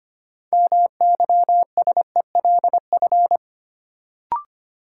Morse code, heard as a keyed tone, about 25 wpm, 700 Hz. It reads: MYSELF
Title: Sentences from Top 1300 Words - New Word Drill - 25wpm